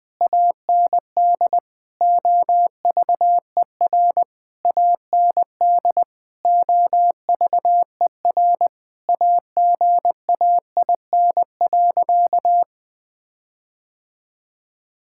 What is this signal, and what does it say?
Morse code, heard as a keyed tone, about 20 wpm, 700 Hz. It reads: AND OVER AND OVER AGAIN.
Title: Sentences with 6-Word Phrases - 20wpm